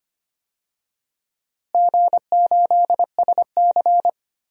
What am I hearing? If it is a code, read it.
Morse code, heard as a keyed tone, about 25 wpm, 700 Hz. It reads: G8SC
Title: DXCC-100 Call Signs - 4 Characters - Repeated 3x - 25wpm